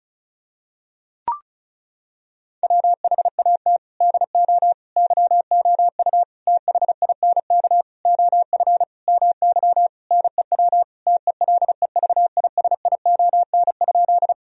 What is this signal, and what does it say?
Morse code, heard as a keyed tone, about 36 wpm, 700 Hz. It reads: WHAT DO YOU THINK OF MY NEW TELEVISION?
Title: Sentences from Top 1000 Words - Review Part 4 - 35wpm